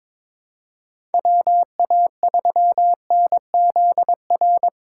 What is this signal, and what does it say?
Morse code, heard as a keyed tone, about 22 wpm, 700 Hz. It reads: WA3NZR
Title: K1USN SST QSO Practice - 22wpm